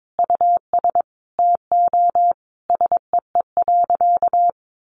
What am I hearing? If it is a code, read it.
Morse code, heard as a keyed tone, about 22 wpm, 700 Hz. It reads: US TO SEE.
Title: Sentences from Top 500 Words 22wpm